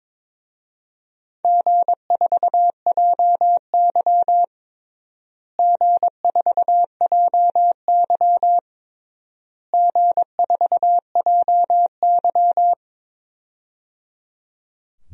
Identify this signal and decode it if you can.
Morse code, heard as a keyed tone, about 22 wpm, 700 Hz. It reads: G4JY G4JY G4JY
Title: DXCC-100 Call Signs - 4 Characters - Repeated 3x - 22wpm